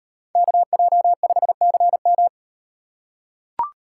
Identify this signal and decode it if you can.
Morse code, heard as a keyed tone, about 34 wpm, 700 Hz. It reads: KJ5CM